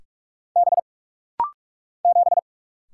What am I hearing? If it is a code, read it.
Morse code, heard as a keyed tone, about 44 wpm, 700 Hz. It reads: B 7
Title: Single Letter-Number Alt - Rapid-Fire 45wpm